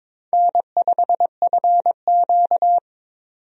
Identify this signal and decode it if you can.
Morse code, heard as a keyed tone, about 22 wpm, 700 Hz. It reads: N5FQ
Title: USA Call Signs - 4 Characters - 22wpm